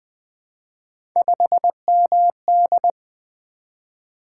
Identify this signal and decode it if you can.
Morse code, heard as a keyed tone, about 20 wpm, 700 Hz. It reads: 5MD
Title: Three Letter-Number Combinations - 20wpm